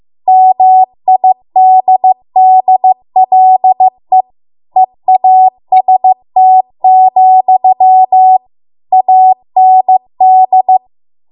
Morse code, heard as a keyed tone, about 15 wpm, 750 Hz. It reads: MIDDLE EAST, AND